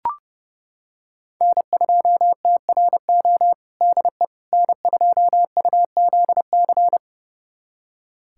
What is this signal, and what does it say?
Morse code, heard as a keyed tone, about 30 wpm, 700 Hz. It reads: N2TRO DE N2UZC
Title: USA Callsign DE Callsign - 5 Characters - 30wpm